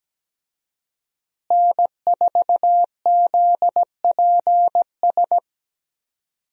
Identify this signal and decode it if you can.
Morse code, heard as a keyed tone, about 17 wpm, 700 Hz. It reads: N4ZPS
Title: USA Call Signs - 5 Characters - Repeated 3x - 17wpm